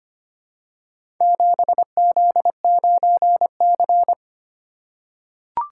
Morse code, heard as a keyed tone, about 25 wpm, 700 Hz. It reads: 7Z9C
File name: practice / 25wpm / Four Letter-Number Combinations - 25wpm